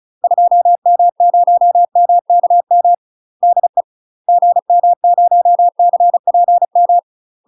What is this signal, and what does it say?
Morse code, heard as a keyed tone, about 36 wpm, 700 Hz. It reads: 2M0MKM DE GM0CPM
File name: UK MASTER QSOs TRAINING / 35WPM / UK MASTER QSOs 0031